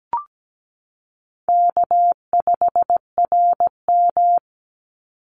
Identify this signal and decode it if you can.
Morse code, heard as a keyed tone, about 17 wpm, 700 Hz. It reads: K5RM
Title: Easy Call Signs Repeated 2x - 17wpm